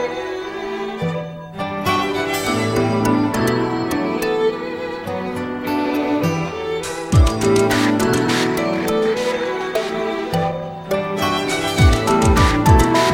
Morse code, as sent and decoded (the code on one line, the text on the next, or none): none